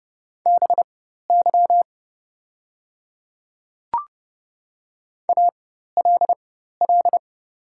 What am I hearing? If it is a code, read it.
Morse code, heard as a keyed tone, about 30 wpm, spaced as for 14 wpm, 700 Hz. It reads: BY ALL
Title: 4x Character Spacing - Top 100 Words - 30wpm